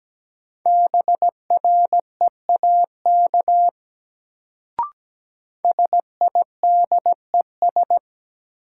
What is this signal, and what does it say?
Morse code, heard as a keyed tone, about 17 wpm, 700 Hz. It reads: BREAK SIDES
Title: Sets of 4 Words - New Word Drill - 17wpm